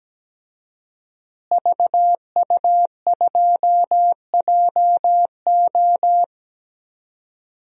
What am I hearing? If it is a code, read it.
Morse code, heard as a keyed tone, about 17 wpm, 700 Hz. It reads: VU2JO
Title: DXCC-100 Call Signs - 5 Characters - Repeated 3x - 17wpm